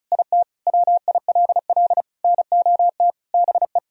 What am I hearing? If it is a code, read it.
Morse code, heard as a keyed tone, about 34 wpm, 700 Hz. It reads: IT WILL NOT BE
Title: Sentences from Top 1000 Words - Review Part 4 - 35wpm